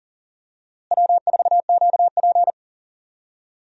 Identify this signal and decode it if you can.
Morse code, heard as a keyed tone, about 40 wpm, 700 Hz. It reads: W4QP
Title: USA Call Signs - 4 Characters - Repeated 3x - 40wpm